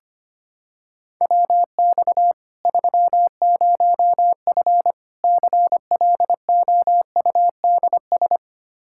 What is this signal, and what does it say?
Morse code, heard as a keyed tone, about 25 wpm, 700 Hz. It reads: WX 30F CLOUDS
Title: Practice QSO - Exchange 2 - Info - 25wpm